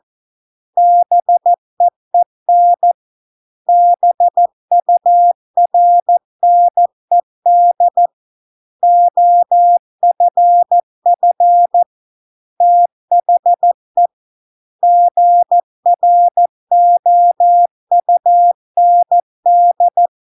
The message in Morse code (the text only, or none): BEEN BURNED OFF THE GROUND